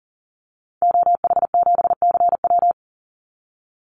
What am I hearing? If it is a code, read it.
Morse code, heard as a keyed tone, about 40 wpm, 700 Hz. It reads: OH7CW
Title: Call Signs Repeated 2x - 40wpm